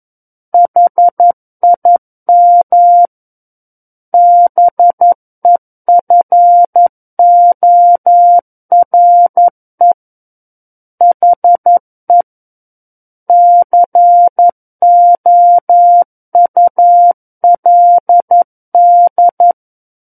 Morse code, heard as a keyed tone, about 11 wpm, 700 Hz. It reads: HIM BEFORE HE COULD
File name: 11 WPM